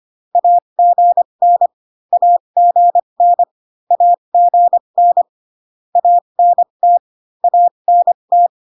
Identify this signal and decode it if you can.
Morse code, heard as a keyed tone, about 25 wpm, spaced as for 22 wpm, 700 Hz. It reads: AGN AGN AGN ANT ANT
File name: CW common abbreviations alphabetical repeated 3 times 25 18wpm